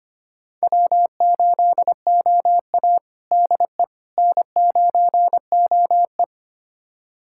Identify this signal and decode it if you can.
Morse code, heard as a keyed tone, about 25 wpm, 700 Hz. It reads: W8OA DE N9OE
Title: USA Callsign DE Callsign - 4 Characters - 25wpm